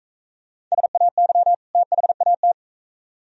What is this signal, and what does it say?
Morse code, heard as a keyed tone, about 42 wpm, 700 Hz. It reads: SAY THAT